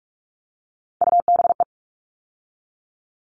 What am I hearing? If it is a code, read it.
Morse code, heard as a keyed tone, about 44 wpm, 700 Hz. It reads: UBE
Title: All Cut Numbers - 3-Digit 45wpm